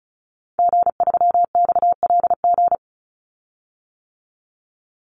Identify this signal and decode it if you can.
Morse code, heard as a keyed tone, about 34 wpm, 700 Hz. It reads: G3XLG
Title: Call Signs - 35wpm